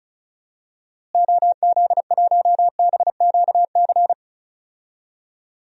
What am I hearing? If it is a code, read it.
Morse code, heard as a keyed tone, about 36 wpm, 700 Hz. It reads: OZ1BQC
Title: DXCC-100 Call Signs - 6 Characters - Repeated 2x - 35wpm